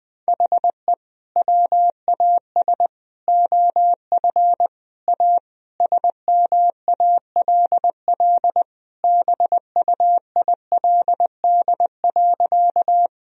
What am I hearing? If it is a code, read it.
Morse code, heard as a keyed tone, about 21 wpm, 700 Hz. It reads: HE WAS OF A SMALL BUILD.
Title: Sentences from Top 1000 Words - Review Part 1 - 20wpm